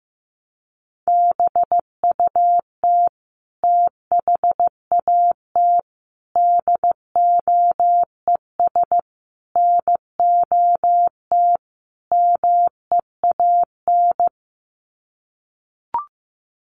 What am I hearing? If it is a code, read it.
Morse code, heard as a keyed tone, about 15 wpm, 700 Hz. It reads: BUT THAT DOES NOT MEAN
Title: Sets of 5 Words 15wpm